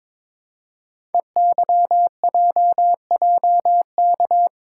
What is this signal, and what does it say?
Morse code, heard as a keyed tone, about 22 wpm, 700 Hz. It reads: EYJJK
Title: Code Groups - 5 Letters - 22wpm